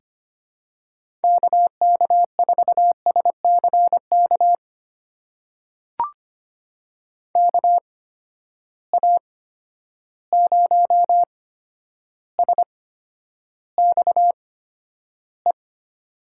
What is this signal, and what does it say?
Morse code, heard as a keyed tone, about 25 wpm, spaced as for 11 wpm, 700 Hz. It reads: KK4SCK KA0SXE
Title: USA Call Signs - 6 Characters - 8x Character Spacing - 25wpm